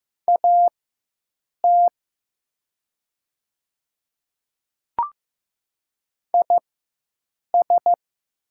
Morse code, heard as a keyed tone, about 15 wpm, spaced as for 7 wpm, 700 Hz. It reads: AT IS
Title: 4x Character Spacing - Top 100 Words - 15wpm